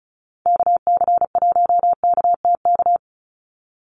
Morse code, heard as a keyed tone, about 36 wpm, 700 Hz. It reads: KC1KTK
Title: USA Call Signs 35wpm